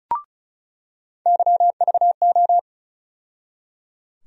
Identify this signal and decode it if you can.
Morse code, heard as a keyed tone, about 36 wpm, 700 Hz. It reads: YVO
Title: USA Call Sign Postfixes - 3 Letters 35wpm